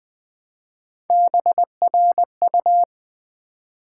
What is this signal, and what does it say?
Morse code, heard as a keyed tone, about 20 wpm, 700 Hz. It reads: BRU
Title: Build A Word - 1 Letter At A Time - 3 to 5-Letter Words - 20wpm